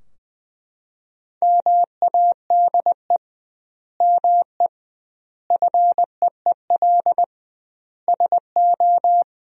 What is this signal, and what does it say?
Morse code, heard as a keyed tone, about 20 wpm, 700 Hz. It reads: MADE ME FEEL SO
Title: 2x Word-Spacing - Sentences from Top 400 Words - 20wpm